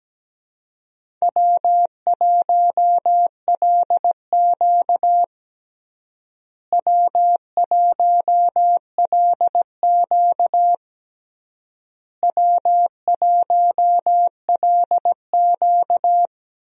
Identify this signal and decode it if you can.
Morse code, heard as a keyed tone, about 17 wpm, 700 Hz. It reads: W1LQ W1LQ W1LQ
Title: USA Call Signs - 4 Characters - Repeated 3x - 17wpm